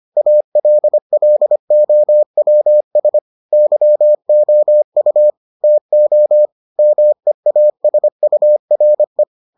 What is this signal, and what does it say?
Morse code, heard as a keyed tone, about 25 wpm, 600 Hz. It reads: ALLOWS YOU TO MEASURE